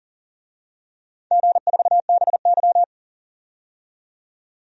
Morse code, heard as a keyed tone, about 40 wpm, 700 Hz. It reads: G4BY
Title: DXCC-100 Call Signs - 4 Characters - 40wpm